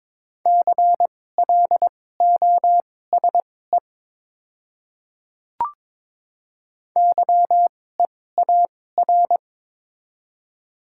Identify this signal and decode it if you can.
Morse code, heard as a keyed tone, about 22 wpm, spaced as for 16 wpm, 690 Hz. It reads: CLOSE YEAR
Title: 2x Character Spacing - Top 300 Words - 22wpm